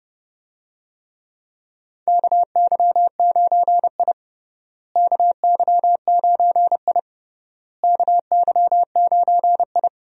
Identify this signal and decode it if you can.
Morse code, heard as a keyed tone, about 31 wpm, 700 Hz. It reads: KY9I KY9I KY9I
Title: USA Call Signs - 4 Characters - Repeated 3x - 30wpm